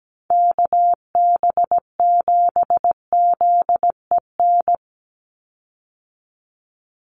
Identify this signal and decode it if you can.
Morse code, heard as a keyed tone, about 17 wpm, 700 Hz. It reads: KB7ZEN
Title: USA Call Signs 17wpm